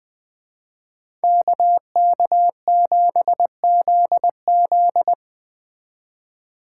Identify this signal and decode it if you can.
Morse code, heard as a keyed tone, about 20 wpm, 700 Hz. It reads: KK7ZZ
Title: Speed Racing - USA 2x2 Call Signs 17wpm